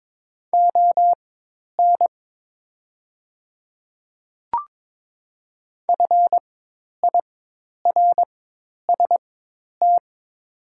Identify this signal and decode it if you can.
Morse code, heard as a keyed tone, about 22 wpm, spaced as for 10 wpm, 700 Hz. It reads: ON FIRST
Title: 4x Character Spacing - Top 100 Words - 22wpm